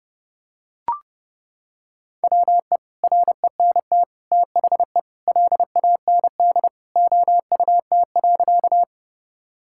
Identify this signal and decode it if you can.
Morse code, heard as a keyed tone, about 30 wpm, 700 Hz. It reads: WE RENT THE LAND OUT.
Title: Sentences from Top 1000 Words - Review Part 5 - 30wpm